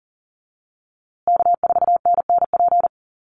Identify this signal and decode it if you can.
Morse code, heard as a keyed tone, about 40 wpm, 700 Hz. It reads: K4NNP
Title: USA Call Signs Repeated 3x 40wpm